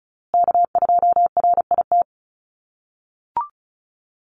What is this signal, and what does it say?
Morse code, heard as a keyed tone, about 34 wpm, 700 Hz. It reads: K2RIT
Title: USA Call Signs Repeated 2x 35wpm